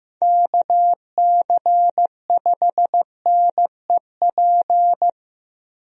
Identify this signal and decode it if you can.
Morse code, heard as a keyed tone, about 16 wpm, 700 Hz. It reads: KC5NEP
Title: USA Call Signs - 6 Characters - Repeated 2x - 15wpm